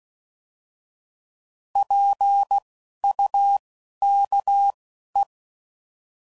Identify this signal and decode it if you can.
Morse code, heard as a keyed tone, about 16 wpm, spaced as for 11 wpm, 800 Hz. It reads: PUKE
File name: Prefixes / By First Letter / Ps